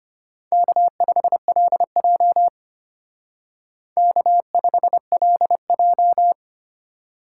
Morse code, heard as a keyed tone, about 29 wpm, 700 Hz. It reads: K5LJ K5LJ
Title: Speed Racing - USA 1x2 Call Signs 25wpm